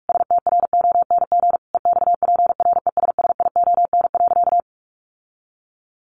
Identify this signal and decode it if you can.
Morse code, heard as a keyed tone, about 44 wpm, 700 Hz. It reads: STRONG EXPRESSION.
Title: Sentences from Top 1200 Words 45wpm